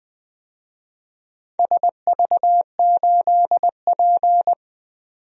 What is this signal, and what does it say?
Morse code, heard as a keyed tone, about 20 wpm, 700 Hz. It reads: SV8P
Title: DXCC-100 Call Signs - 4 Characters - 20wpm